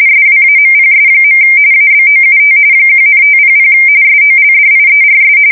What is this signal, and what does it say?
RTTY, about 45.45 baud. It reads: SWAMPS. ABOUT 1,500 SOLDIERS
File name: fsk170 / usnavy2